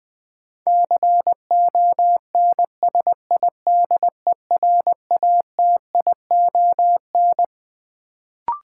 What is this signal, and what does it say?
Morse code, heard as a keyed tone, about 21 wpm, 700 Hz. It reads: CONSIDERATION
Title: Sentences from Top 1500 Words - New Word Drill - 20wpm